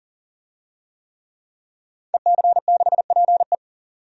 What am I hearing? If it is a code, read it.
Morse code, heard as a keyed tone, about 39 wpm, 700 Hz. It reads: EC6PE
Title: DXCC-100 Call Signs - 5 Characters - 40wpm